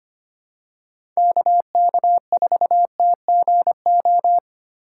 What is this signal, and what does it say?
Morse code, heard as a keyed tone, about 25 wpm, 700 Hz. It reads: KK4TGO
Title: USA Call Signs - 6 Characters - 3x Character Spacing - 25wpm